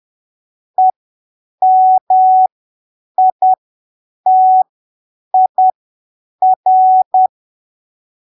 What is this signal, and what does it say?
Morse code, heard as a keyed tone, about 10 wpm, spaced as for 7 wpm, 750 Hz. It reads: EMITIR
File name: arquivos / texto / texto 16 10ppm